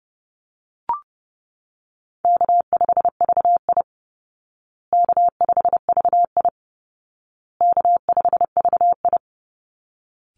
Morse code, heard as a keyed tone, about 30 wpm, 700 Hz. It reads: K5VI K5VI K5VI
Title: Easy Call Signs Repeated 3x - 30wpm